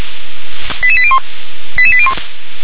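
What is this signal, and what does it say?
Five-tone selective call (ZVEI), heard as quick runs of five tones, tones of about 70 ms. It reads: 88801 88801
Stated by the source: ZVEI five-tone selective call received on an SDR.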